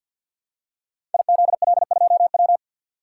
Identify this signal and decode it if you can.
Morse code, heard as a keyed tone, about 51 wpm, 700 Hz. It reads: IZL2W
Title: Code Groups - 5 Letter:Numbers - 50wpm